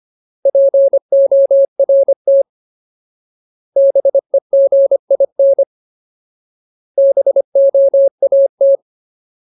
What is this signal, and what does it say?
Morse code, heard as a keyed tone, about 25 wpm, 550 Hz. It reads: PORT BEGIN BOAT
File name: Ultimate Word List / 25WPM Ultimate